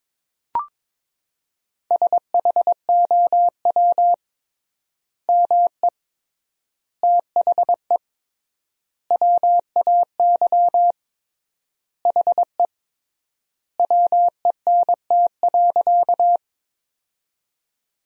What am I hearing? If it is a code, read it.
Morse code, heard as a keyed tone, about 22 wpm, 700 Hz. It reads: SHOW ME THE WAY HE WENT.
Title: 3x Word-Spacing - Sentences from Top 200 Words - 22wpm